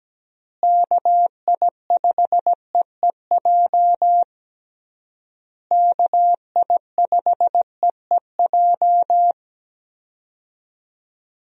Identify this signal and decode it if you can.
Morse code, heard as a keyed tone, about 17 wpm, 700 Hz. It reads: KI5EEJ KI5EEJ
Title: USA Call Signs - 6 Characters - Repeated 3x - 17wpm